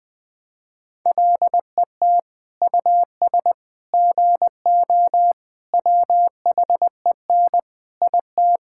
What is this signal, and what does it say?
Morse code, heard as a keyed tone, about 20 wpm, 700 Hz. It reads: LET US GO WHEN IT
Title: Sentences with 2-Word Phrases - Spoken Beginning - 20wpm